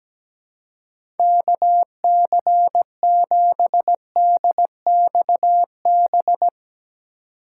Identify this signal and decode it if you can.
Morse code, heard as a keyed tone, about 17 wpm, 700 Hz. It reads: KC7DXB
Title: USA Call Signs - 4x Character Spacing - 17wpm